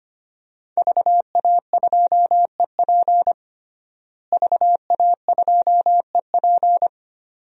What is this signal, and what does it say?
Morse code, heard as a keyed tone, about 25 wpm, 700 Hz. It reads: VA2EP VA2EP
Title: DXCC-100 Call Signs - 5 Characters - Repeated 2x - 25wpm